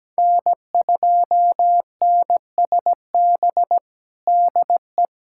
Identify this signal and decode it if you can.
Morse code, heard as a keyed tone, about 17 wpm, 700 Hz. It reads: N2NSB DE
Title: USA Callsign DE Callsign - 5 Characters - 17wpm